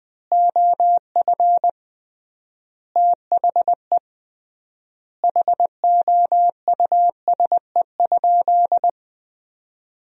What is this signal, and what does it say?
Morse code, heard as a keyed tone, about 20 wpm, 700 Hz. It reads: OF THE HOUSE?
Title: 3x Word-Spacing - Sentences from Top 200 Words - 20wpm